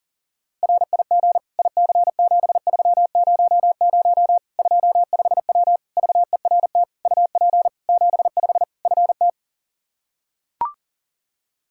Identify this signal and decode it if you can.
Morse code, heard as a keyed tone, about 40 wpm, 700 Hz. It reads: RIG IC7300 25W VERT UP 75 FT
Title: Practice QSO - Exchange 2 - Info - 40wpm